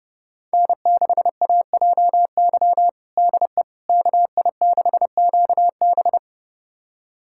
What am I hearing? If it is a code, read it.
Morse code, heard as a keyed tone, about 30 wpm, 700 Hz. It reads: N6AJY DE KI6QB
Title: USA Callsign DE Callsign - 5 Characters - 30wpm